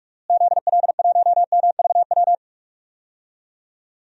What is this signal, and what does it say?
Morse code, heard as a keyed tone, about 47 wpm, 700 Hz. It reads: ZL1MVW
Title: DXCC-100 Call Signs - 6 Characters - Repeated 3x - 45wpm